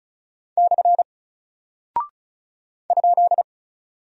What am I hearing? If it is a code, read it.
Morse code, heard as a keyed tone, about 36 wpm, 700 Hz. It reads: / ?
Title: Single Character (Letter:Number:Punctuation:Prosign) 35wpm